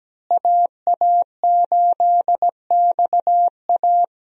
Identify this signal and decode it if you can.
Morse code, heard as a keyed tone, about 17 wpm, 700 Hz. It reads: AA8XA
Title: USA Call Signs - 5 Characters - Repeated 3x - 17wpm